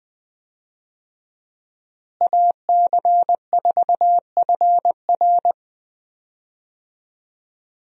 Morse code, heard as a keyed tone, about 20 wpm, 700 Hz. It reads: AC4FR
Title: USA Call Signs - 5 Characters - Repeated 2x - 20wpm